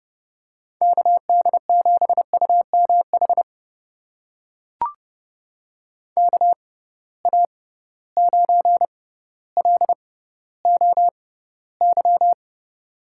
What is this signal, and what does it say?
Morse code, heard as a keyed tone, about 30 wpm, spaced as for 15 wpm, 700 Hz. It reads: KD7UMH KA9LOY
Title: USA Call Signs - 6 Characters - 6x Character Spacing - 30wpm